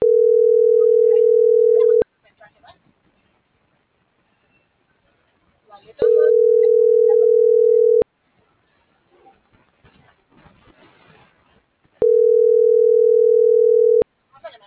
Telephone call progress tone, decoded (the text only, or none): ringback tone